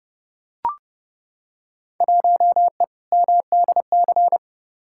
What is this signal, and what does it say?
Morse code, heard as a keyed tone, about 30 wpm, 700 Hz. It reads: 1E MDC
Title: Field Day Exchanges - 30wpm